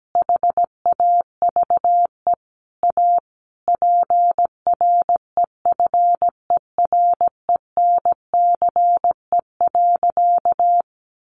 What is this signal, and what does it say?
Morse code, heard as a keyed tone, about 17 wpm, 700 Hz. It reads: HAVE A PREFERENCE.
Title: Sentences from Top 1500 Words 17wpm